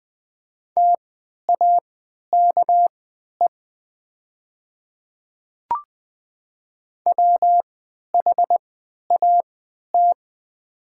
Morse code, heard as a keyed tone, about 20 wpm, spaced as for 11 wpm, 700 Hz. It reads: TAKE WHAT
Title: 3x Character Spacing - Top 300 Words - 20wpm